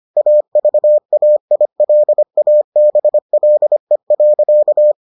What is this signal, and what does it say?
Morse code, heard as a keyed tone, about 25 wpm, 600 Hz. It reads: AVAILABLE.